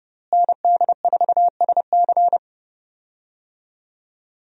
Morse code, heard as a keyed tone, about 29 wpm, 700 Hz. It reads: ND4SC
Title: USA Call Signs - 5 Characters - Repeated 2x - 30wpm